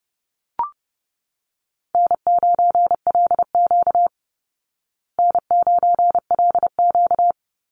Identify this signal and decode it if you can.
Morse code, heard as a keyed tone, about 30 wpm, 700 Hz. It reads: N9LQ N9LQ
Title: Call Signs Repeated 2x - 30wpm